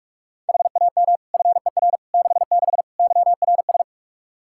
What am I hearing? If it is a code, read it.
Morse code, heard as a keyed tone, about 45 wpm, 700 Hz. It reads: HAM FER 66 YRS
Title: Practice QSO - Exchange 3 - More Info - 45wpm